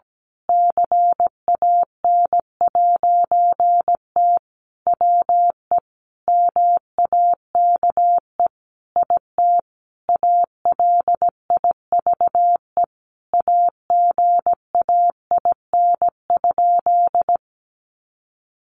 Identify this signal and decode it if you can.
Morse code, heard as a keyed tone, about 17 wpm, 700 Hz. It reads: CAN'T WE MAKE IT ALIVE AGAIN?
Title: A Sound Of Thunder 17wpm